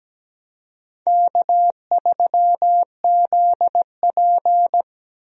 Morse code, heard as a keyed tone, about 17 wpm, 700 Hz. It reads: K3ZP
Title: USA Call Signs - 4 Characters - Repeated 2x - 17wpm